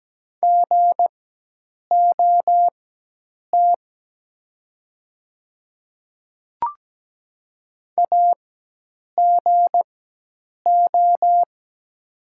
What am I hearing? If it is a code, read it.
Morse code, heard as a keyed tone, about 17 wpm, spaced as for 8 wpm, 700 Hz. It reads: GOT AGO
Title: 4x Character Spacing - Top 500 Words - 17wpm